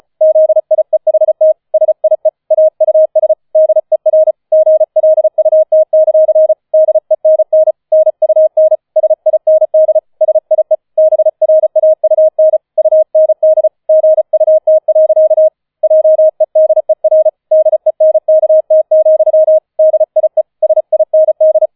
Morse code, heard as a keyed tone, about 36 wpm, 630 Hz. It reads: ZIEHT SIE AUS DER GLUT; DENN NUN SIND SIE BRAUN UND GUT. JEDER DENKT, DIE SIND